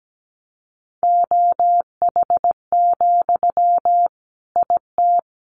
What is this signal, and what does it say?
Morse code, heard as a keyed tone, about 17 wpm, 700 Hz. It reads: OH, IT
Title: Sentences from Top 1000 Words 17wpm